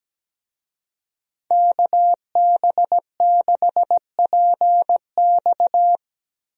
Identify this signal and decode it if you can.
Morse code, heard as a keyed tone, about 17 wpm, 700 Hz. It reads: KB6PX
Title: USA Call Signs - 5 Characters - Repeated 3x - 17wpm